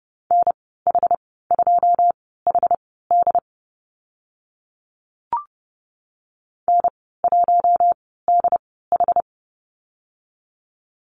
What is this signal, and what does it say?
Morse code, heard as a keyed tone, about 30 wpm, spaced as for 17 wpm, 700 Hz. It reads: NH2HD N1DH
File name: Lesson 14g - Number 2 - Call Signs 30wpm